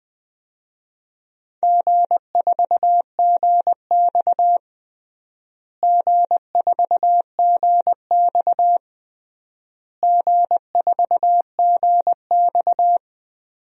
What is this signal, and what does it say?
Morse code, heard as a keyed tone, about 21 wpm, 700 Hz. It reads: G4GX G4GX G4GX
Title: DXCC-100 Call Signs - 4 Characters - Repeated 3x - 20wpm